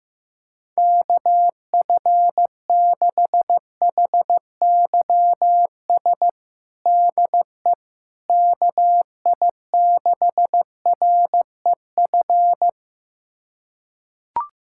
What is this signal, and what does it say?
Morse code, heard as a keyed tone, about 15 wpm, 700 Hz. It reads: KF6HYS DE KI6REF
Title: USA Callsign DE Callsign - 6 Characters - 15wpm